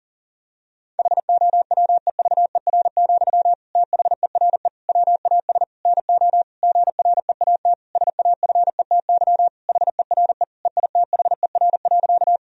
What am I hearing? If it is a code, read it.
Morse code, heard as a keyed tone, about 40 wpm, 700 Hz. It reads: HOWEVER, THERE WAS NO GREAT SAFETY HERE EITHER.
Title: Sentences from Top 1000 Words - Review Part 4 - 40wpm